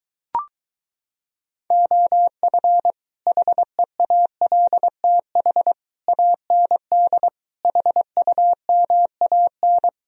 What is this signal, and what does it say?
Morse code, heard as a keyed tone, about 23 wpm, 700 Hz. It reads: OF HEALTH AND HUMAN